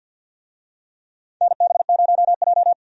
Morse code, heard as a keyed tone, about 53 wpm, 700 Hz. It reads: NB0J
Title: USA Call Signs - 4 Characters - 50wpm